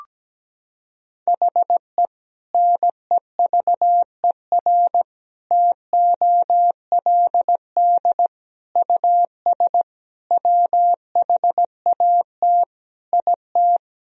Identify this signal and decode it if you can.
Morse code, heard as a keyed tone, about 17 wpm, 700 Hz. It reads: HE NEVER TOLD US WHAT IT
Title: Sentences from Top 400 Words and Abbreviations 17wpm